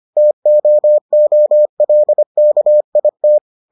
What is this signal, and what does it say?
Morse code, heard as a keyed tone, about 25 wpm, 600 Hz. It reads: TOOLKIT